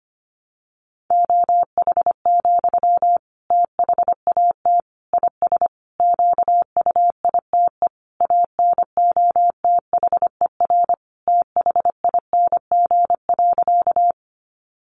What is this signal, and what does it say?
Morse code, heard as a keyed tone, about 25 wpm, 700 Hz. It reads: OH, THAT IS QUITE ANOTHER THING.
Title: Sentences from Top 700 Words 25wpm